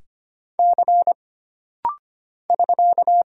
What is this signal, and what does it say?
Morse code, heard as a keyed tone, about 25 wpm, 700 Hz. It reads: C <SK>
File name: Single Character (Letter:Number:Punctuation:Prosign) - Rapid-Fire - 25wpm